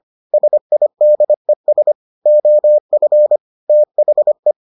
Morse code, heard as a keyed tone, about 25 wpm, 600 Hz. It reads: SIDES OF THE